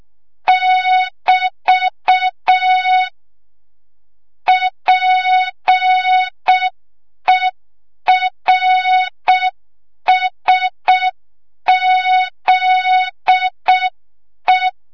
Morse code, heard as a keyed tone, about 6 wpm, 3700 Hz. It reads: <BT> PERSZE
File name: Morze-HA2MN / hang / 25 GYAKORLAT